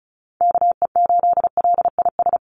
Morse code, heard as a keyed tone, about 34 wpm, 700 Hz. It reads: KE8LIS
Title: USA Call Signs 35wpm